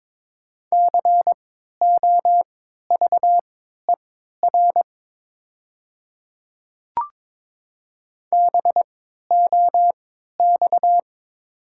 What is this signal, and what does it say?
Morse code, heard as a keyed tone, about 22 wpm, spaced as for 13 wpm, 700 Hz. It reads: COVER BOX
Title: 3x Character Spacing - Top 500 Words - 22wpm